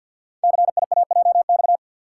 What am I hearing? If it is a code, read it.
Morse code, heard as a keyed tone, about 48 wpm, 700 Hz. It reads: CIAJX
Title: Code Groups - 5 Letters - 50wpm